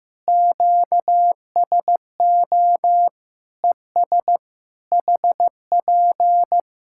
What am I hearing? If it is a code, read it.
Morse code, heard as a keyed tone, about 15 wpm, 700 Hz. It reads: QSO ES HP
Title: Practice QSO - Exchange 4 - Ending - 15wpm